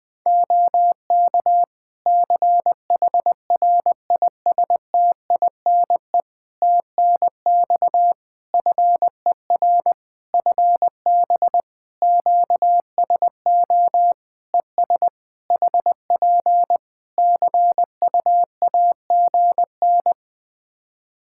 Morse code, heard as a keyed tone, about 19 wpm, 700 Hz. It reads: OK CHRISTINE TNX FER FB QSO ES HP CUAGN
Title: Practice QSO - Exchange 4 - Ending - 20wpm